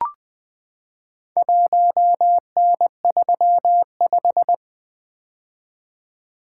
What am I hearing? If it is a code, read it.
Morse code, heard as a keyed tone, about 20 wpm, 700 Hz. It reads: 1N35
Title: Four Letter-Number Combinations - 20wpm